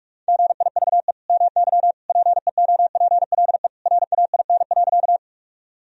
Morse code, heard as a keyed tone, about 46 wpm, 700 Hz. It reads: GIVE MY PEOPLE RAIN.
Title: Sentences from Top 1000 Words - Review Part 1 - 45wpm